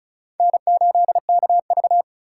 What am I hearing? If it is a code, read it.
Morse code, heard as a keyed tone, about 34 wpm, 700 Hz. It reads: N8KV
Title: USA Call Signs - 4 Characters - 3x Character Spacing - 35wpm